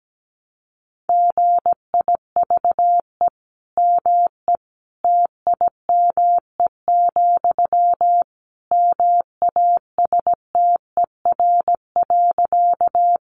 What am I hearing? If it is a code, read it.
Morse code, heard as a keyed tone, about 17 wpm, 700 Hz. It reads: GIVE ME TIME, MASTER.
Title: Sentences from Top 1300 Words 17wpm